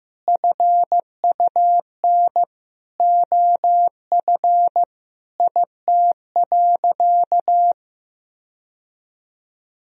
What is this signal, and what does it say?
Morse code, heard as a keyed tone, about 16 wpm, 700 Hz. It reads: FUN OF IT.
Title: Sentences from Top 1000 Words - Review Part 4 - 15wpm